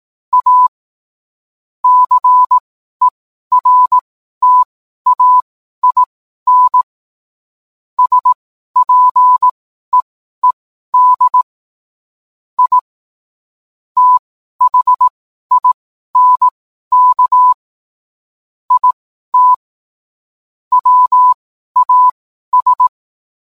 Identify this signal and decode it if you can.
Morse code, heard as a keyed tone, about 18 wpm, spaced as for 12 wpm, 1000 Hz. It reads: A CERTAIN SPEED I THINK IT WAS
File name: cwblog / cw mp3 directory / mem morse 12